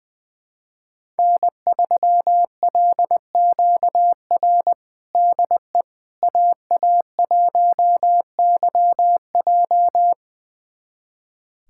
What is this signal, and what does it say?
Morse code, heard as a keyed tone, about 21 wpm, 700 Hz. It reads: N3LQR DE AA1YJ
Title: USA Callsign DE Callsign - 5 Characters - 20wpm